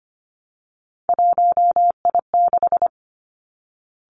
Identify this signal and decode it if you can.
Morse code, heard as a keyed tone, about 25 wpm, 700 Hz. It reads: 1I6
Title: Three Challenging Letter-Number Combinations 25wpm